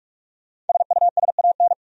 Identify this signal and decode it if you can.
Morse code, heard as a keyed tone, about 46 wpm, 700 Hz. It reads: SUSAN
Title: USA Common Names - Max 5 Letters - 45wpm